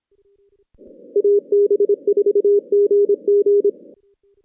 Morse code, heard as a keyed tone, about 24 wpm, 410 Hz. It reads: AB4GG
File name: 2016 JAN CW NAQP / 2016 Jan naqp k5ot - 0-8